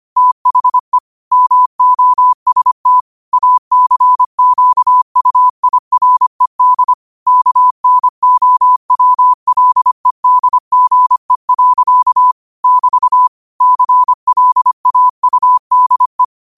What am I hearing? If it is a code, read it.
Morse code, heard as a keyed tone, about 25 wpm, 1000 Hz. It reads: THE MOST ACQUIRED KNOWLEDGE. <BT> CLAUDE
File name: 25wpm